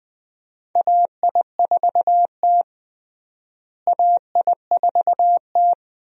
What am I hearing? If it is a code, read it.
Morse code, heard as a keyed tone, about 20 wpm, 700 Hz. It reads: AI4T AI4T
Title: USA Call Signs - 4 Characters - Repeated 2x - 20wpm